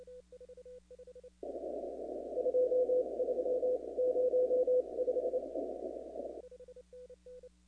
Morse code, heard as a keyed tone, about 28 wpm, spaced as for 24 wpm, 500 Hz. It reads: 23 05
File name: contests / 2019-cw-wpx-kt5j-k5pi- -2330 0-37